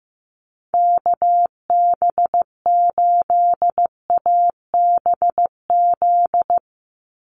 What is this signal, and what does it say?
Morse code, heard as a keyed tone, about 15 wpm, 700 Hz. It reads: KB8ABZ
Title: USA Call Signs 15wpm